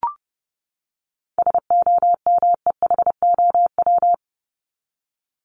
Morse code, heard as a keyed tone, about 30 wpm, 700 Hz. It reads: SOMEHOW
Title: Top 1501-1750 Words 30wpm